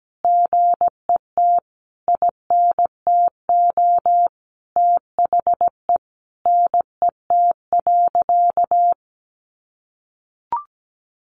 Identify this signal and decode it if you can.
Morse code, heard as a keyed tone, about 17 wpm, 700 Hz. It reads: GET INTO THE NET.
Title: Sentences from Top 1400 Words 17wpm